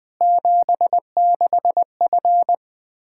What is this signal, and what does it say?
Morse code, heard as a keyed tone, about 20 wpm, 700 Hz. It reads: 76F
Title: K1USN SST QSO Practice 20wpm